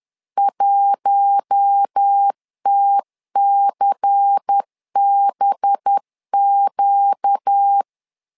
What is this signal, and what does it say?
Morse code, heard as a keyed tone, about 10 wpm, 800 Hz. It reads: 1TCBQ
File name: LEZIONI / 5 LBVC / 5 LBVC 10wpm 2349